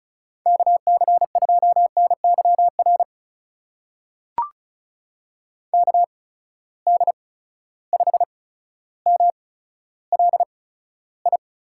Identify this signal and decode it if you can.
Morse code, heard as a keyed tone, about 34 wpm, spaced as for 15 wpm, 700 Hz. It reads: KC2NYR KD5MLI
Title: USA Call Signs - 6 Characters - 8x Character Spacing - 35wpm